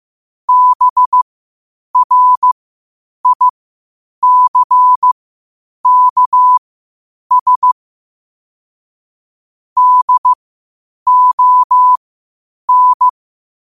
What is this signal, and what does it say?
Morse code, heard as a keyed tone, about 15 wpm, spaced as for 8 wpm, 1000 Hz. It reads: BRICKS DON